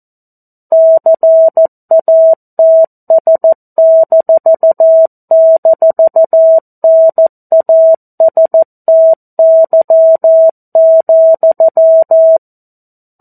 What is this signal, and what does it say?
Morse code, heard as a keyed tone, about 14 wpm, 650 Hz. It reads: CATS--NASTY,
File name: 02-Chapter-02 14 WPM